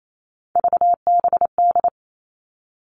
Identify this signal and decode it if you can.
Morse code, heard as a keyed tone, about 28 wpm, 700 Hz. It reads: VBD
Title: All Cut Numbers - 3-Digit 28wpm